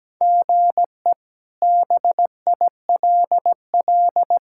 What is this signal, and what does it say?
Morse code, heard as a keyed tone, about 17 wpm, 700 Hz. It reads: GE BILL
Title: K1USN SST QSO Practice - 17wpm